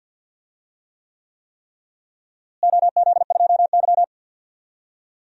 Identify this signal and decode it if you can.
Morse code, heard as a keyed tone, about 53 wpm, 700 Hz. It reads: OZ2Y
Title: DXCC-100 Call Signs - 4 Characters - Repeated 3x - 50wpm